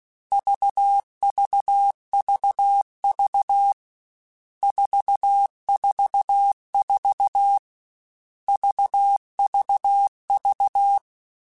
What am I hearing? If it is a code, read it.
Morse code, heard as a keyed tone, about 16 wpm, 800 Hz. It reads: VVVV 444 VVV
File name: Groups@13 / Groups@13 mp3s / VB46@13